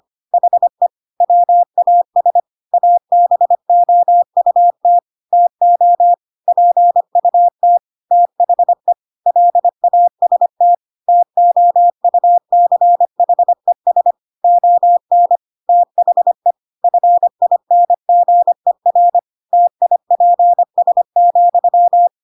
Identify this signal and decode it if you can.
Morse code, heard as a keyed tone, about 25 wpm, 700 Hz. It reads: HE WAS ABOUT TO PUT THE LAST TOUCHES ON THE FINGER TIPS,
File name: Pinocchio / Pinocchio ch3 25 25 ik1hge